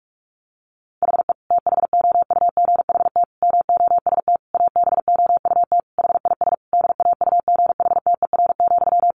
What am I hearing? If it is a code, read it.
Morse code, heard as a keyed tone, about 44 wpm, 700 Hz. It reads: HE THOUGHT MOST ABOUT HIS DAUGHTER,
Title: Sentences from Top 1500 Words 45wpm